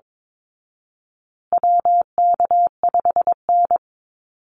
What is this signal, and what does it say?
Morse code, heard as a keyed tone, about 22 wpm, 700 Hz. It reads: WK5N
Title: Call Signs Repeated 2x - 22wpm